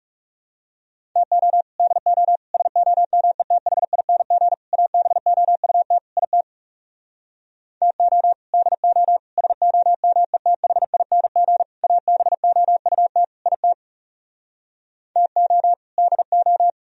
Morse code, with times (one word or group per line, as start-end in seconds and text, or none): 1.2-1.6 TO
1.8-2.4 DO
2.5-4.5 SOMETHING
4.7-6.0 ABOUT
6.2-6.4 IT
7.8-8.3 TO
8.5-9.2 DO
9.4-11.6 SOMETHING
11.8-13.2 ABOUT
13.5-13.7 IT
15.2-15.7 TO
16.0-16.7 DO